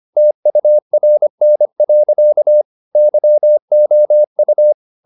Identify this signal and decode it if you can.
Morse code, heard as a keyed tone, about 25 wpm, 600 Hz. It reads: TURN. YOU